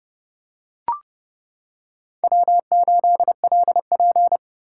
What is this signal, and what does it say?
Morse code, heard as a keyed tone, about 28 wpm, 700 Hz. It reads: W8LP